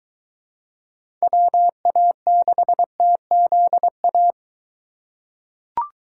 Morse code, heard as a keyed tone, about 21 wpm, 700 Hz. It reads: WA6TZA